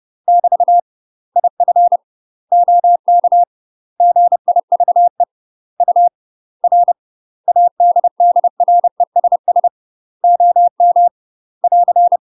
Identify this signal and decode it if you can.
Morse code, heard as a keyed tone, about 30 wpm, 700 Hz. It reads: <BT> IF OK GIVE UR ADDRESS OM <AR>
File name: UK MASTER QSOs TRAINING/30WPM